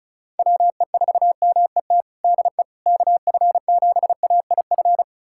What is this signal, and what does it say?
Morse code, heard as a keyed tone, about 34 wpm, 700 Hz. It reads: WE4MET DE KF7AIF
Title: USA Callsign DE Callsign - 6 Characters - 35wpm